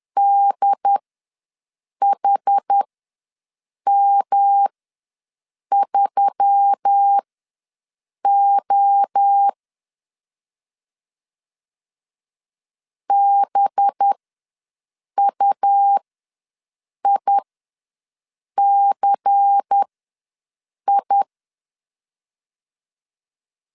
Morse code, heard as a keyed tone, about 10 wpm, spaced as for 6 wpm, 800 Hz. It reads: DHM3O BUICI
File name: LEZIONI / 5 LBVC / 5 LBVC ws 10wpm 3733